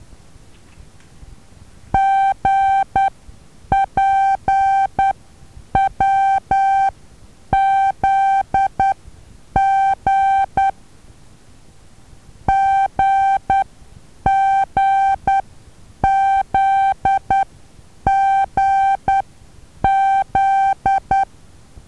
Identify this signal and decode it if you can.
Morse code, heard as a keyed tone, about 10 wpm, spaced as for 8 wpm, 780 Hz. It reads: GPWZG GGZGZ
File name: Taviro HA6VA B